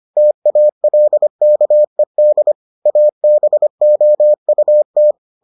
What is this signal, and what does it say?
Morse code, heard as a keyed tone, about 25 wpm, 600 Hz. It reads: TALKED ABOUT